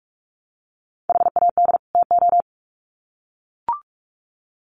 Morse code, heard as a keyed tone, about 44 wpm, 700 Hz. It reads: HAD TO
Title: Sets of 2 Words 45wpm